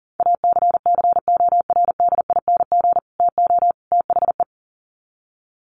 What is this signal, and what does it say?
Morse code, heard as a keyed tone, about 37 wpm, 700 Hz. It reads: ACCORDING TO THE
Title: Sets of 3 Words 40wpm